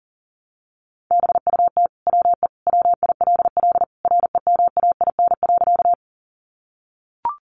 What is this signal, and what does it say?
Morse code, heard as a keyed tone, about 40 wpm, 700 Hz. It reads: BUT WE WILL REMAIN.
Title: Sentences from Top 1200 Words 40wpm